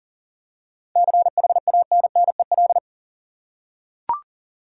Morse code, heard as a keyed tone, about 39 wpm, 700 Hz. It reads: CHANNEL
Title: Seven Letter Words - 40wpm